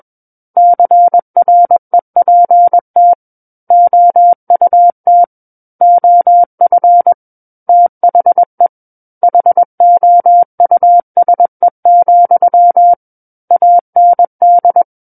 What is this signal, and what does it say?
Morse code, heard as a keyed tone, about 21 wpm, 700 Hz. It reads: CREPT OUT OF THE HOUSE, AND
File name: Book One / 12-Chapter 12 - 21 WPM